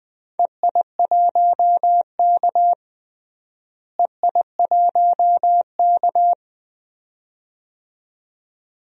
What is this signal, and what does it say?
Morse code, heard as a keyed tone, about 20 wpm, 700 Hz. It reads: EI1K EI1K
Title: DXCC-100 Call Signs - 4 Characters - Repeated 2x - 20wpm